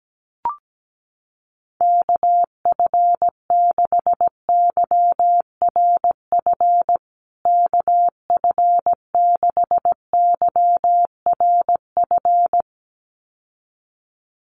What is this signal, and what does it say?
Morse code, heard as a keyed tone, about 17 wpm, 700 Hz. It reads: KF6YRF KF6YRF
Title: USA Call Signs Repeated 2x 17wpm